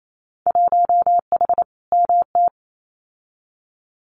Morse code, heard as a keyed tone, about 28 wpm, 700 Hz. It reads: 1H MT
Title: Winter Field Day Exchanges Encore 28wpm